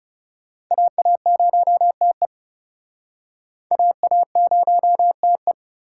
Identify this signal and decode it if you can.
Morse code, heard as a keyed tone, about 42 wpm, 700 Hz. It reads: AA0TE AA0TE